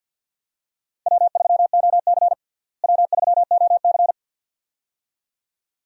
Morse code, heard as a keyed tone, about 51 wpm, 700 Hz. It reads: W3OC W3OC
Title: USA Call Signs - 4 Characters - Repeated 2x - 50wpm